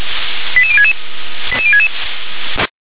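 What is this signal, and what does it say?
Five-tone selective call (ZVEI), heard as quick runs of five tones, tones of about 70 ms. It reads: 88066 88066
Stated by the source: ZVEI five-tone selective call 88066, received on SDR.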